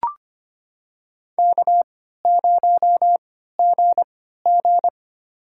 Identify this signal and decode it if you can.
Morse code, heard as a keyed tone, about 25 wpm, spaced as for 14 wpm, 700 Hz. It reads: K0GG
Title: USA Call Signs - 4 Characters - 3x Character Spacing - 25wpm